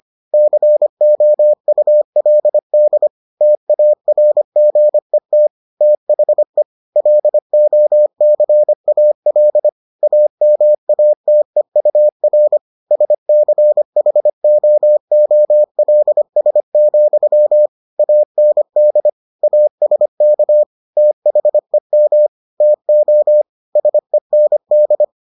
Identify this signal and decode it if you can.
Morse code, heard as a keyed tone, about 25 wpm, 600 Hz. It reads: COULD TARGET THE LOCAL AMATEUR SCHOOLS, AND ASK THEM TO SEND